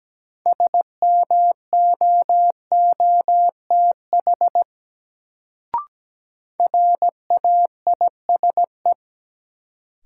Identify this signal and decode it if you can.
Morse code, heard as a keyed tone, about 17 wpm, 700 Hz. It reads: SMOOTH RAISE
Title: Sentences from Top 1400 Words - New Word Drill Alt - 17wpm